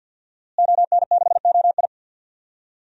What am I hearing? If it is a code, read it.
Morse code, heard as a keyed tone, about 52 wpm, 700 Hz. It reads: ON6OI
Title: DXCC-100 Call Signs - 5 Characters - Repeated 2x - 50wpm